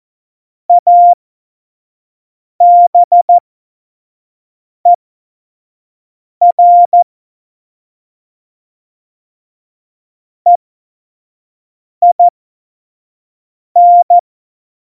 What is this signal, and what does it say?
Morse code, heard as a keyed tone, about 14 wpm, spaced as for 5 wpm, 700 Hz. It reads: ABER EIN
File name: Morsekurs / Witze / Witz 22. Dezember 2020, 25 F70